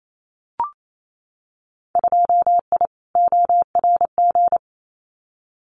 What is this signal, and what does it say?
Morse code, heard as a keyed tone, about 28 wpm, 700 Hz. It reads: 2I ORG
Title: Winter Field Day Exchanges Encore 28wpm